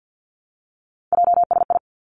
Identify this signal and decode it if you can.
Morse code, heard as a keyed tone, about 50 wpm, 700 Hz. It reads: JSI
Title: Three Challenging Letter-Number Combinations 50wpm